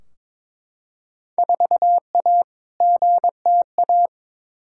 Morse code, heard as a keyed tone, about 22 wpm, 700 Hz. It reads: 4A GTA
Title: Field Day Exchanges Encore - 22wpm